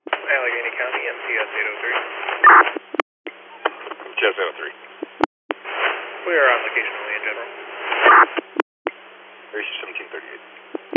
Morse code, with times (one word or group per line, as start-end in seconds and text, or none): none